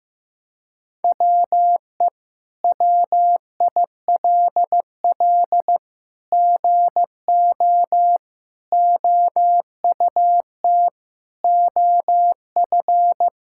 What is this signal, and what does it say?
Morse code, heard as a keyed tone, about 15 wpm, 700 Hz. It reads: WE WILL GO OUT OF
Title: Sentences from Top 1000 Words - Review Part 6 - 15wpm